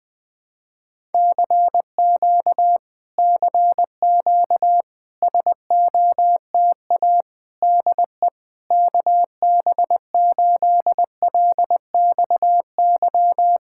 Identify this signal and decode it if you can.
Morse code, heard as a keyed tone, about 20 wpm, 700 Hz. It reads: CQ CQ SOTA DE KB8LXY
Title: Speed-Racing Common QSO Phrases - 17wpm